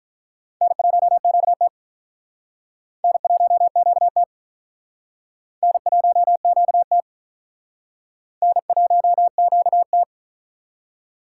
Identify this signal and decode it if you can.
Morse code, heard as a keyed tone, about 44 wpm, 700 Hz. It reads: N1QT N1QT N1QT N1QT